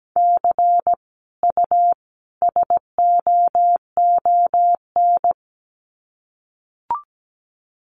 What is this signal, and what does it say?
Morse code, heard as a keyed tone, about 17 wpm, 700 Hz. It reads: C U SOON
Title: Common QSO Phrases 17wpm